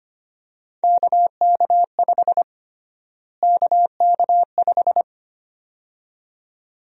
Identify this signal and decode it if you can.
Morse code, heard as a keyed tone, about 25 wpm, 700 Hz. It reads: KK5 KK5
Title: USA Call Sign Prefixes - Repeated 2x - 25wpm